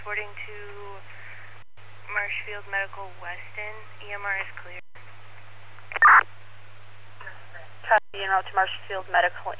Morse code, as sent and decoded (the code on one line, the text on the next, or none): none